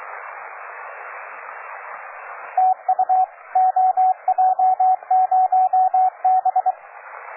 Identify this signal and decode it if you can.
Morse code, heard as a keyed tone, about 23 wpm, 750 Hz, with radio noise. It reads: TU OJ0B
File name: cw forum / OJ0B sur 1823 le 21-09-2008 17h00 test GP XXL